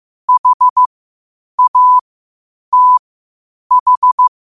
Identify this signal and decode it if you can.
Morse code, heard as a keyed tone, about 15 wpm, spaced as for 8 wpm, 1000 Hz. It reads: HATH
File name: 8wpm